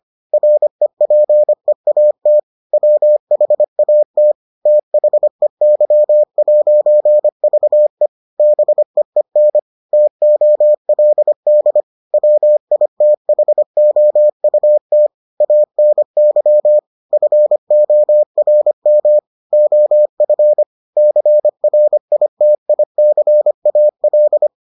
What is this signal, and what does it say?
Morse code, heard as a keyed tone, about 25 wpm, 600 Hz. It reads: REPEAT WHAT THEY'VE BEEN TOLD WITHOUT ANY FORM OF CRITICAL